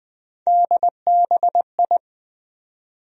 Morse code, heard as a keyed tone, about 20 wpm, 700 Hz. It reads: DBI
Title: USA Call Sign Postfixes - 3 Letters 20wpm